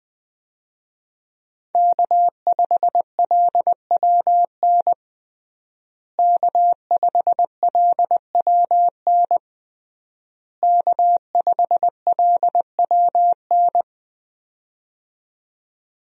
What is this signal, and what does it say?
Morse code, heard as a keyed tone, about 21 wpm, 700 Hz. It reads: K5LWN K5LWN K5LWN
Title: USA Call Signs - 5 Characters - Repeated 3x - 20wpm